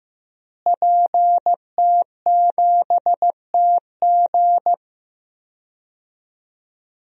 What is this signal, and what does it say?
Morse code, heard as a keyed tone, about 15 wpm, 700 Hz. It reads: PT7TG
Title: DXCC-100 Call Signs - 5 Characters - Repeated 2x - 15wpm